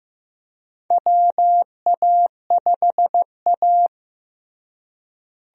Morse code, heard as a keyed tone, about 15 wpm, 700 Hz. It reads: WA5A
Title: USA Call Signs - 4 Characters - Repeated 2x - 15wpm